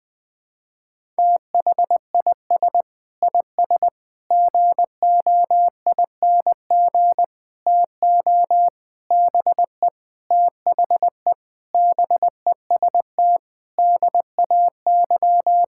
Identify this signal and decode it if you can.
Morse code, heard as a keyed tone, about 21 wpm, 700 Hz. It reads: THIS IS GOING TO BE THE BEST DAY
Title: Sentences with 5-Word Phrases - 20wpm